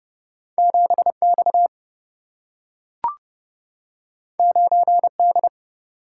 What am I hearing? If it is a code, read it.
Morse code, heard as a keyed tone, about 31 wpm, 700 Hz. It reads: 7 X 9 D
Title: Single-Letter Single-Number 30wpm